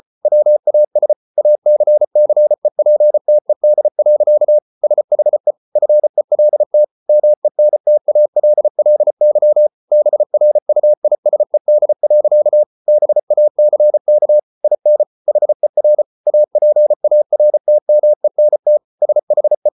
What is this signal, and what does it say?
Morse code, heard as a keyed tone, about 34 wpm, 600 Hz. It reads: WAS ACCEPTED. SHE FELT MENTALLY BRUISED. BACK IN HER APARTMENT SHE